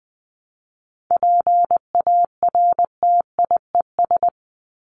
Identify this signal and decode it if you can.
Morse code, heard as a keyed tone, about 20 wpm, 700 Hz. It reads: PARTIES
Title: ES:S Suffix - Top 100 Words with Base Words 20wpm